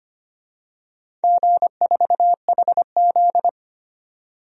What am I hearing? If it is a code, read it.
Morse code, heard as a keyed tone, about 25 wpm, 700 Hz. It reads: G4HZ
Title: DXCC-100 Call Signs - 4 Characters - 25wpm